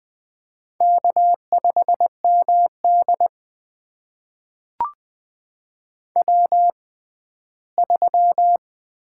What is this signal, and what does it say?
Morse code, heard as a keyed tone, about 21 wpm, spaced as for 14 wpm, 700 Hz. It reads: K5MD W3
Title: USA Call Signs - 4 Characters - 6x Character Spacing - 20wpm